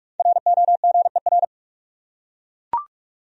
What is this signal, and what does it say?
Morse code, heard as a keyed tone, about 42 wpm, 700 Hz. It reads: ROGER
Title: USA Common Names - Max 5 Letters - 45wpm